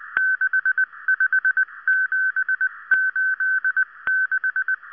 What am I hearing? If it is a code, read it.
Morse code, heard as a keyed tone, about 19 wpm, 1550 Hz. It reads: 65786